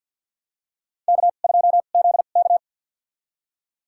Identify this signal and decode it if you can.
Morse code, heard as a keyed tone, about 52 wpm, spaced as for 37 wpm, 700 Hz. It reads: K2ZK
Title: USA Call Signs - 4 Characters - 2x Character Spacing - 50wpm